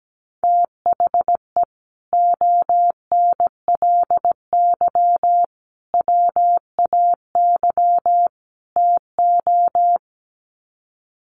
Morse code, heard as a keyed tone, about 17 wpm, 700 Hz. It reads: THE ONLY WAY TO
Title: Sets of 4 Words 17wpm